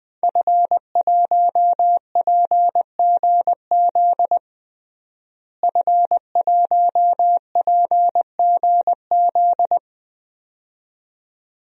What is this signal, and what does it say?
Morse code, heard as a keyed tone, about 21 wpm, 700 Hz. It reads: F1PGZ F1PGZ
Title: DXCC-100 Call Signs - 5 Characters - Repeated 2x - 20wpm